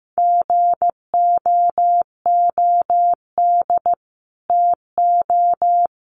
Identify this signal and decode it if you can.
Morse code, heard as a keyed tone, about 15 wpm, 700 Hz. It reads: GOOD TO
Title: Sentences from Top 500 Words 15wpm